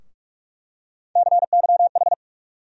Morse code, heard as a keyed tone, about 46 wpm, 700 Hz. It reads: CYH
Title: USA Call Sign Postfixes - 3 Letters 45wpm